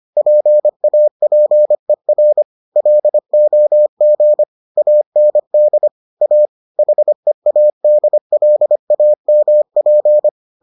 Morse code, heard as a keyed tone, about 25 wpm, 600 Hz. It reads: PAPER LOG AND A HEADLAMP